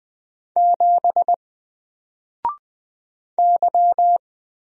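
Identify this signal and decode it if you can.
Morse code, heard as a keyed tone, about 20 wpm, 700 Hz. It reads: 7 Y
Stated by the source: Single letter and number Morse drill at 20 wpm, from N4LCD.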